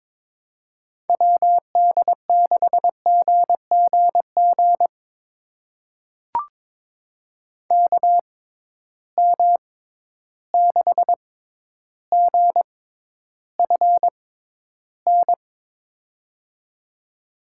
Morse code, heard as a keyed tone, about 22 wpm, spaced as for 11 wpm, 700 Hz. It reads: WD6GGG KM6GFN
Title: USA Call Signs - 6 Characters - 6x Character Spacing - 22wpm